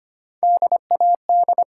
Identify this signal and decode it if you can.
Morse code, heard as a keyed tone, about 25 wpm, 700 Hz. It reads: DAD
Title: Three Letter Words - 25wpm